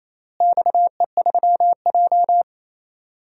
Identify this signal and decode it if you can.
Morse code, heard as a keyed tone, about 28 wpm, 700 Hz. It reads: XE3J
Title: DXCC-100 Call Signs - 4 Characters - Repeated 2x - 28wpm